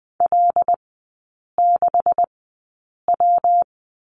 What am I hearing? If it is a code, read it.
Morse code, heard as a keyed tone, about 20 wpm, spaced as for 8 wpm, 700 Hz. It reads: L6W
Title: 138 Lesson 23e - Letter f - Three Characters 20wpm